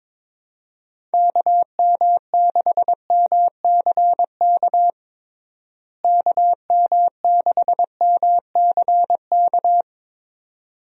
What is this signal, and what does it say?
Morse code, heard as a keyed tone, about 22 wpm, 700 Hz. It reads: KM6MCK KM6MCK
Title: USA Call Signs - 6 Characters - Repeated 2x - 22wpm